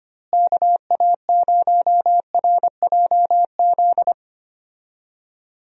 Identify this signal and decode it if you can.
Morse code, heard as a keyed tone, about 25 wpm, 700 Hz. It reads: KA0RJZ
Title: USA Call Signs - 6 Characters - Repeated 3x - 25wpm